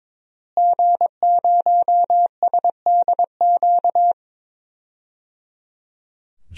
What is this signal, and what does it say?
Morse code, heard as a keyed tone, about 22 wpm, 700 Hz. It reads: G0SDQ
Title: DXCC-100 Call Signs - 5 Characters - Repeated 3x - 22wpm